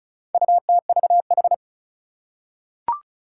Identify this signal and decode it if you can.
Morse code, heard as a keyed tone, about 36 wpm, 700 Hz. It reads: UTVH
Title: Four Challenging Letter-Number Combinations 35wpm